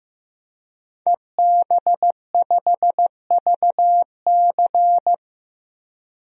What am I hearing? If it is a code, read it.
Morse code, heard as a keyed tone, about 15 wpm, 700 Hz. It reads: EB5VC
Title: DXCC-100 Call Signs - 5 Characters - 15wpm